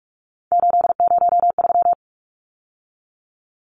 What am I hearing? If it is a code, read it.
Morse code, heard as a keyed tone, about 44 wpm, 700 Hz. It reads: 803
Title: Numbers - 3-Digit 45wpm